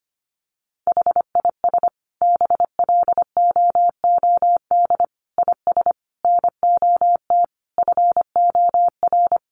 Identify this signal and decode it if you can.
Morse code, heard as a keyed tone, about 25 wpm, 700 Hz. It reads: HIS BLOOD IS NOT FOR
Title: Sentences from Top 1400 Words 25wpm